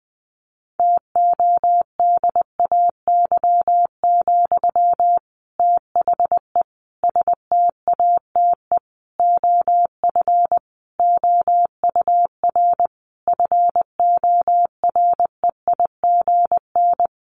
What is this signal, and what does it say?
Morse code, heard as a keyed tone, about 20 wpm, 700 Hz. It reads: TODAY, THE STATE OF OUR FOREIGN
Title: Sentences from Top 1400 Words 20wpm